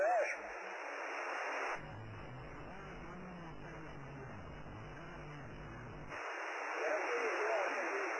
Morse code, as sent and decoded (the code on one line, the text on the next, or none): none